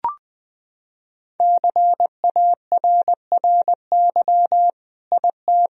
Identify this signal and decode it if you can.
Morse code, heard as a keyed tone, about 20 wpm, 700 Hz. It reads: CARRY IT
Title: Sentences from Top 1000 Words - Review Part 1 - 20wpm